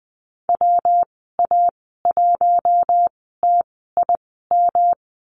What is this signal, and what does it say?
Morse code, heard as a keyed tone, about 20 wpm, spaced as for 15 wpm, 700 Hz. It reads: WA1TIM
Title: 122 Lesson 20g - Number 3 - Call Signs 20wpm